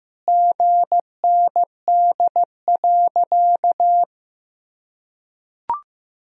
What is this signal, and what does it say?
Morse code, heard as a keyed tone, about 15 wpm, 700 Hz. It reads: GND.
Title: Sentences from Top 500 Words and Abbreviations 15wpm